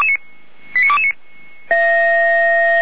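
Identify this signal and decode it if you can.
Five-tone selective call (ZVEI), heard as quick runs of five tones, tones of about 70 ms. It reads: 89229
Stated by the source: ZVEI five-tone selective call received by SDR, from feuerwehr-orlamuende.org fire brigade audio archive.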